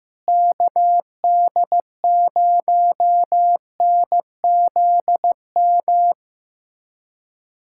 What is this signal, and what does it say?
Morse code, heard as a keyed tone, about 15 wpm, 700 Hz. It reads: KD0NZM
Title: USA Callsign DE Callsign - 6 Characters - 15wpm